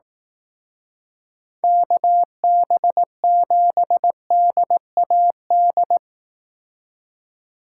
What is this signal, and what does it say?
Morse code, heard as a keyed tone, about 18 wpm, 700 Hz. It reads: KB7DAD
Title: Speed Racing - USA 2x3 Call Signs 15wpm